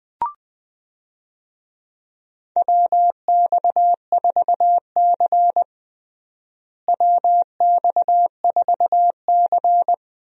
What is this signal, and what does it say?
Morse code, heard as a keyed tone, about 20 wpm, 700 Hz. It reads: WX4C WX4C
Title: USA Call Signs - 4 Characters - Repeated 2x - 20wpm